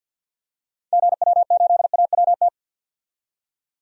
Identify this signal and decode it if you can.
Morse code, heard as a keyed tone, about 45 wpm, 700 Hz. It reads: GW8AWT
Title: DXCC-100 Call Signs - 6 Characters - 50wpm